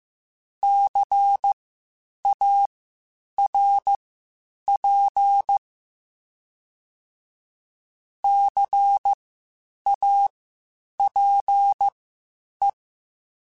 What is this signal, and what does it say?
Morse code, heard as a keyed tone, about 15 wpm, spaced as for 8 wpm, 800 Hz. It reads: CARP CAPE
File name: Spaced Alphas / Spaced Alphas mp3s / Cs sp @13